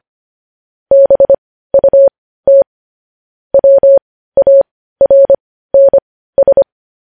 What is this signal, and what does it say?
Morse code, heard as a keyed tone, about 25 wpm, spaced as for 15 wpm, 550 Hz. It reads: BUT WARNS